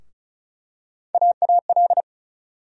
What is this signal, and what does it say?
Morse code, heard as a keyed tone, about 36 wpm, 700 Hz. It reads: AAL
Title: USA Call Sign Postfixes - 3 Letters 35wpm